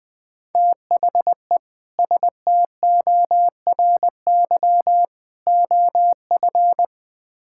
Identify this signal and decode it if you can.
Morse code, heard as a keyed tone, about 20 wpm, 700 Hz. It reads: THE STORY OF
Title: Build A Phrase - 3-Word Phrases - 20wpm